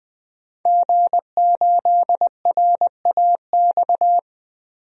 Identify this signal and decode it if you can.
Morse code, heard as a keyed tone, about 20 wpm, 700 Hz. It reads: G8RAX
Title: DXCC-100 Call Signs - 5 Characters - 20wpm